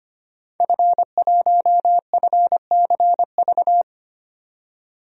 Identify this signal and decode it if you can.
Morse code, heard as a keyed tone, about 25 wpm, 700 Hz. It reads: F1FCV
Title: DXCC-100 Call Signs - 5 Characters - 25wpm